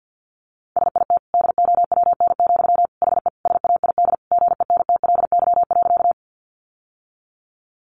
Morse code, heard as a keyed tone, about 50 wpm, 700 Hz. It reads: SIT DOWN, HE SAID GENTLY.
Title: Sentences from Top 1500 Words 50wpm